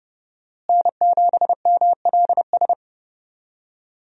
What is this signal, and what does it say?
Morse code, heard as a keyed tone, about 30 wpm, 700 Hz. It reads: N7MLS
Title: USA Call Signs - 5 Characters - Repeated 2x - 30wpm